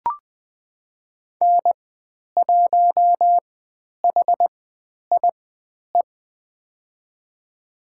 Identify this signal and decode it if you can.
Morse code, heard as a keyed tone, about 20 wpm, spaced as for 10 wpm, 700 Hz. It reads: N1HIE
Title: 109 Lesson 18g - Letter m - Call Signs 20wpm